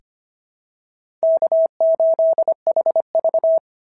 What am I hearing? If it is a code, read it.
Morse code, heard as a keyed tone, about 25 wpm, 690 Hz. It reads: K8HV
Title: SOTA QSO Practice 25wpm